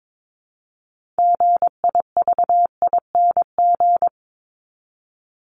Morse code, heard as a keyed tone, about 22 wpm, 700 Hz. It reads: GIVING
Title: ING Suffix - Top 100 Words with Base Words 22wpm